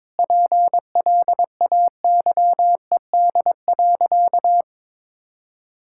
Morse code, heard as a keyed tone, about 22 wpm, 700 Hz. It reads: PLAYED.
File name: practice / 22wpm / Sentences with 3-Word Phrases - 22wpm